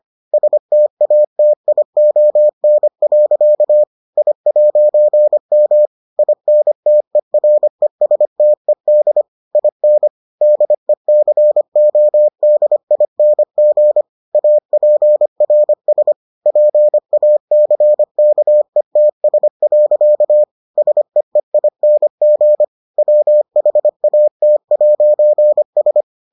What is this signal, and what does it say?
Morse code, heard as a keyed tone, about 25 wpm, 600 Hz. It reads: STATION. I'M INTERESTED IN DECODING APRS PACKETS. SEEING WHAT'S